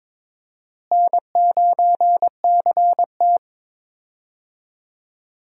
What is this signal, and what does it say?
Morse code, heard as a keyed tone, about 22 wpm, 700 Hz. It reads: N9CT
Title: USA Call Signs - 4 Characters - Repeated 3x - 22wpm